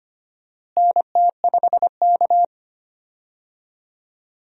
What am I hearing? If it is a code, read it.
Morse code, heard as a keyed tone, about 25 wpm, 700 Hz. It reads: NT5K
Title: USA Call Signs - 4 Characters - Repeated 2x - 25wpm